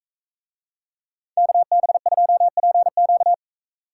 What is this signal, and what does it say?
Morse code, heard as a keyed tone, about 43 wpm, 700 Hz. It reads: KB2PQ